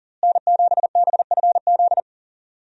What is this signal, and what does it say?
Morse code, heard as a keyed tone, about 37 wpm, 700 Hz. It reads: N7BFZ
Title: USA Call Signs - 5 Characters - Repeated 2x - 40wpm